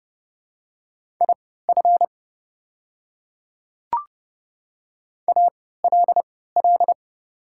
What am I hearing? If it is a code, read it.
Morse code, heard as a keyed tone, about 30 wpm, spaced as for 17 wpm, 700 Hz. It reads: IF ALL
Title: 3x Character Spacing - Top 100 Words - 30wpm